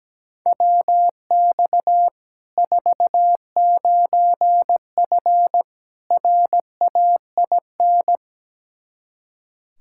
Morse code, heard as a keyed tone, about 17 wpm, 700 Hz. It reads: WX 49F RAIN
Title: Practice QSO - Exchange 2 - Info - 17wpm